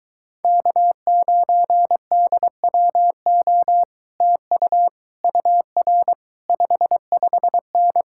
Morse code, heard as a keyed tone, about 23 wpm, 700 Hz. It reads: K9DWO TU UR 55N